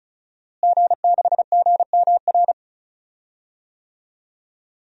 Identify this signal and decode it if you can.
Morse code, heard as a keyed tone, about 36 wpm, 700 Hz. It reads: G6GMR
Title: DXCC-100 Call Signs - 5 Characters - Repeated 2x - 35wpm